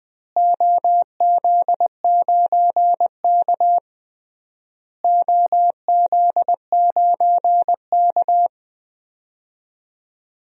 Morse code, heard as a keyed tone, about 20 wpm, 700 Hz. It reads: OZ9K OZ9K
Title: DXCC-100 Call Signs - 4 Characters - Repeated 3x - 20wpm